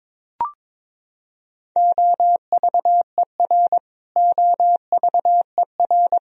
Morse code, heard as a keyed tone, about 22 wpm, 700 Hz. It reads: OVER OVER
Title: Top 100 Words Repeated 22wpm